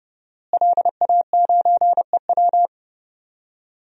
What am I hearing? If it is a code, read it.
Morse code, heard as a keyed tone, about 30 wpm, 700 Hz. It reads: LA9EW
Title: DXCC-100 Call Signs - 5 Characters - Repeated 2x - 30wpm